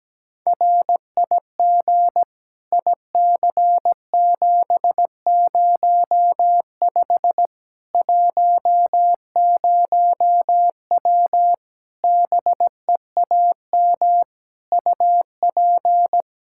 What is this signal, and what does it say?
Morse code, heard as a keyed tone, about 17 wpm, 700 Hz. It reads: RIG IC705 10W BEAM UP
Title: Practice QSO - Exchange 2 - Info - 17wpm